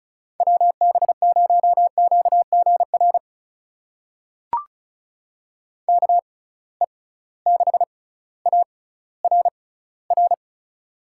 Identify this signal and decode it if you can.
Morse code, heard as a keyed tone, about 36 wpm, spaced as for 18 wpm, 700 Hz. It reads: WB0QGR KE6ARR
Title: USA Call Signs - 6 Characters - 6x Character Spacing - 35wpm